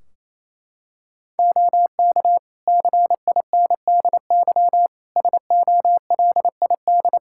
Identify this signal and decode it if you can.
Morse code, heard as a keyed tone, about 28 wpm, 700 Hz. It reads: OK CINDY SOLID
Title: K1USN SST QSO Practice 28wpm